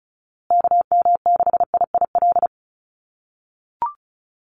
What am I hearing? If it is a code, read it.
Morse code, heard as a keyed tone, about 34 wpm, 700 Hz. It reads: KM6IIL
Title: USA Call Signs Repeated 2x 35wpm